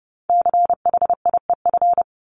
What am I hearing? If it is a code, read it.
Morse code, heard as a keyed tone, about 30 wpm, 700 Hz. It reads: CHIEF
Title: LY Suffix - Top 100 Words with Base Words 30wpm